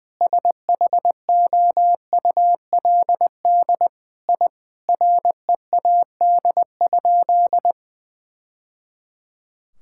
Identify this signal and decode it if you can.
Morse code, heard as a keyed tone, about 20 wpm, 700 Hz. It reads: SHOULD I READ?
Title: Sentences from Top 1000 Words - Review Part 5 - 20wpm